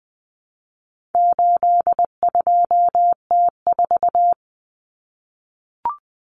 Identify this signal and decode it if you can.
Morse code, heard as a keyed tone, about 20 wpm, 700 Hz. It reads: 82T4
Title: Common Cut Numbers - 4-Digit 20wpm